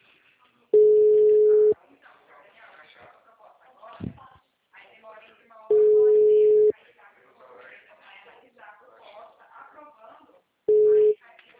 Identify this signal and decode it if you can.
Ringback tone (Europe, 425 Hz), with speech.